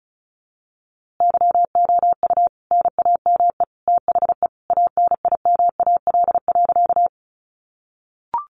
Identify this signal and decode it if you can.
Morse code, heard as a keyed tone, about 34 wpm, 700 Hz. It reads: YOU NAME THE ANIMAL.
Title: A Sound Of Thunder 35wpm